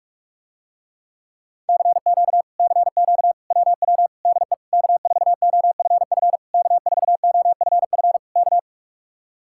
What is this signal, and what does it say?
Morse code, heard as a keyed tone, about 50 wpm, 700 Hz. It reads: CQ CQ WW DE K4OFF K4OFF K